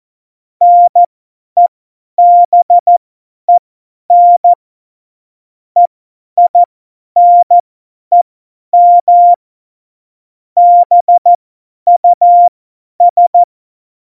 Morse code, heard as a keyed tone, about 14 wpm, spaced as for 10 wpm, 700 Hz. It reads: NEBEN EINEM BUS